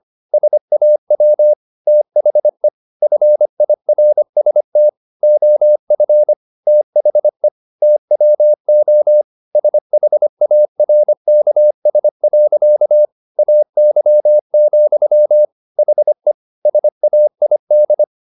Morse code, heard as a keyed tone, about 25 wpm, 600 Hz. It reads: SAW THE FIRST OF THE TWO SHARKS. AY, HE SAID